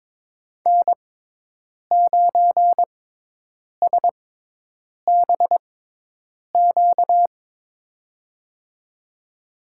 Morse code, heard as a keyed tone, about 22 wpm, spaced as for 8 wpm, 700 Hz. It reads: N9SBQ
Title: USA Call Signs - 6x Character Spacing - 22wpm